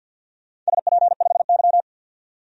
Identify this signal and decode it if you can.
Morse code, heard as a keyed tone, about 53 wpm, 700 Hz. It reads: SP5Y
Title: DXCC-100 Call Signs - 4 Characters - 50wpm